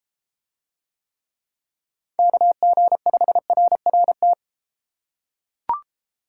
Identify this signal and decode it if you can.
Morse code, heard as a keyed tone, about 33 wpm, 700 Hz. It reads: KG5RRT